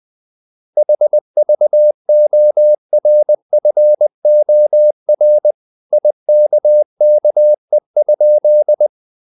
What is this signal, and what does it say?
Morse code, heard as a keyed tone, about 20 wpm, 600 Hz. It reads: HVORFOR IKKE?
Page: Ferie 20 wpm